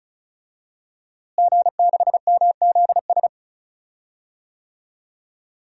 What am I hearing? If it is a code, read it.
Morse code, heard as a keyed tone, about 36 wpm, 700 Hz. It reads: G6MZS
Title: DXCC-100 Call Signs - 5 Characters - Repeated 2x - 35wpm